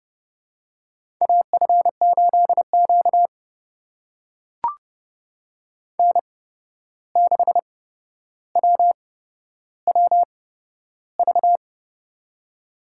Morse code, heard as a keyed tone, about 30 wpm, spaced as for 12 wpm, 700 Hz. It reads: AF8Q N6WWV
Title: USA Call Signs - 8x Character Spacing - 30wpm